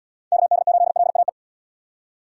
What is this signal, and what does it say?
Morse code, heard as a keyed tone, about 71 wpm, 700 Hz. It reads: KD8LAE